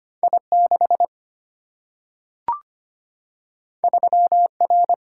Morse code, heard as a keyed tone, about 25 wpm, 700 Hz. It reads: I6 3R
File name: Two Letter-Number Combinations - 25wpm